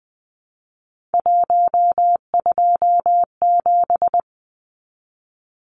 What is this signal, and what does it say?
Morse code, heard as a keyed tone, about 20 wpm, 700 Hz. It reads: 127
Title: Numbers - 3-Digit 20wpm